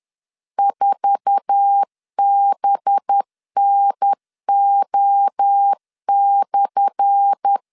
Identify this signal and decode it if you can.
Morse code, heard as a keyed tone, about 10 wpm, 800 Hz. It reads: 4BNO/
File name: LEZIONI / 10 SIGNS / 10A SLASH 10wpm 2808